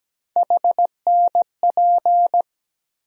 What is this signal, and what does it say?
Morse code, heard as a keyed tone, about 17 wpm, 700 Hz. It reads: HNP
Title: USA Call Sign Postfixes - 3 Letters 17wpm